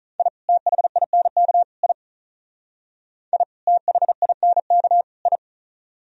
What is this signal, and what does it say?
Morse code, heard as a keyed tone, about 45 wpm, 700 Hz. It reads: I THINK I I THINK I